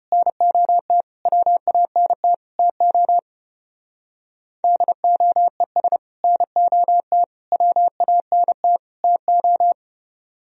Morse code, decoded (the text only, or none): NOT WANT TO DOES NOT WANT TO